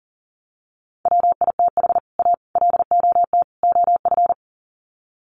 Morse code, heard as a keyed tone, about 40 wpm, 700 Hz. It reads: WITH A LOT OF
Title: Sets of 4 Words 40wpm